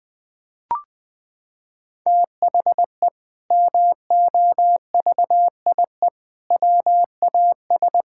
Morse code, heard as a keyed tone, about 21 wpm, 700 Hz. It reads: THE MOVIE WAS
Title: Sentences from Top 1000 Words - Review Part 5 - 20wpm